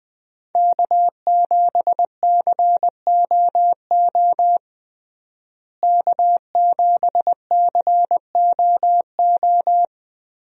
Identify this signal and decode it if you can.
Morse code, heard as a keyed tone, about 21 wpm, 700 Hz. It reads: K7COO K7COO
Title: USA Call Signs - 5 Characters - Repeated 3x - 20wpm